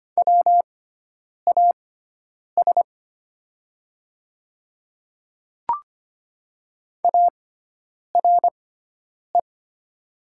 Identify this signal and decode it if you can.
Morse code, heard as a keyed tone, about 25 wpm, spaced as for 9 wpm, 700 Hz. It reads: WAS ARE
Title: 6x Character Spacing - Top 500 Words - 25wpm